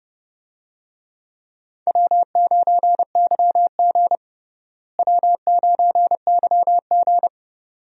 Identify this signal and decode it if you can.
Morse code, heard as a keyed tone, about 31 wpm, 700 Hz. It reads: W9YG W9YG
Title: USA Call Signs - 4 Characters - Repeated 2x - 30wpm